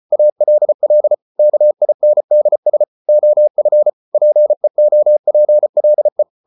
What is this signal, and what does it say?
Morse code, heard as a keyed tone, about 34 wpm, 600 Hz. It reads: ALL KINDS OF PEOPLE